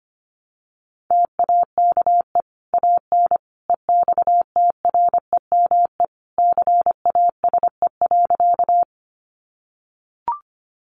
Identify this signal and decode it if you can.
Morse code, heard as a keyed tone, about 25 wpm, 700 Hz. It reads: TAKE AN EXTREME CASE.
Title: Sentences from Top 1300 Words 25wpm